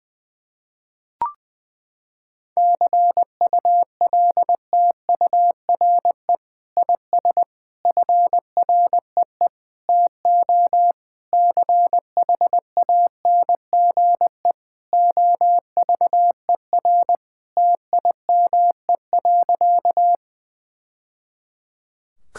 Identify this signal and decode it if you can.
Morse code, heard as a keyed tone, about 20 wpm, 700 Hz. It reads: CULTURE IS FREE TO CHANGE OVER TIME.
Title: Sentences from Top 1000 Words - Review Part 6 - 20wpm